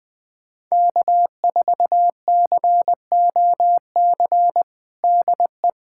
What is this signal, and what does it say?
Morse code, heard as a keyed tone, about 21 wpm, 700 Hz. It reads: K4COC DE
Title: USA Callsign DE Callsign 20wpm